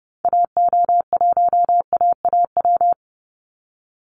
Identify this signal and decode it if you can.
Morse code, heard as a keyed tone, about 30 wpm, 700 Hz. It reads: AO1AAW
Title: Call Signs - 30wpm